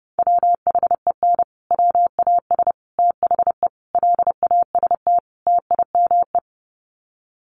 Morse code, heard as a keyed tone, about 30 wpm, 700 Hz. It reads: WHEN WAS THE LAST TIME
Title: Sets of 5 Words Encore 30wpm